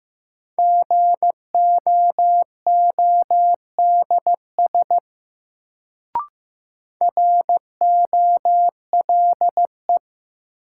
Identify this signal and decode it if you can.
Morse code, heard as a keyed tone, about 15 wpm, 700 Hz. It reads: GOODS ROLE
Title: Sentences from Top 1000 Words - Review Part 6 - New Word Drill - 15wpm